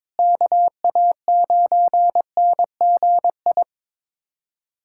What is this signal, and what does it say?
Morse code, heard as a keyed tone, about 22 wpm, 700 Hz. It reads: KA9NGI
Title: USA Callsign DE Callsign 22wpm